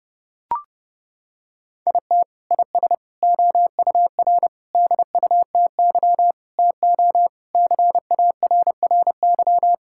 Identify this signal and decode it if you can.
Morse code, heard as a keyed tone, about 30 wpm, 700 Hz. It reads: IT IS OUR DUTY TO CARRY
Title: Sentences from Top 1600 Words 30wpm